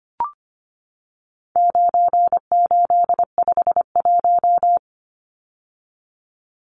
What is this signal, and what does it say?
Morse code, heard as a keyed tone, about 25 wpm, 700 Hz. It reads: 9851
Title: Numbers - 4-Digit 25wpm